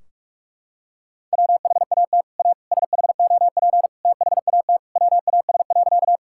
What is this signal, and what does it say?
Morse code, heard as a keyed tone, about 41 wpm, 700 Hz. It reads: WHAT A SHOP THAT WAS.
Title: Sentences from Top 1600 Words 45wpm